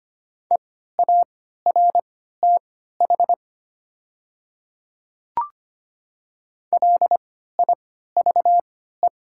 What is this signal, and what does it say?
Morse code, heard as a keyed tone, about 25 wpm, spaced as for 14 wpm, 690 Hz. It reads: EARTH LIVE
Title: 3x Character Spacing - Top 200 Words - 25wpm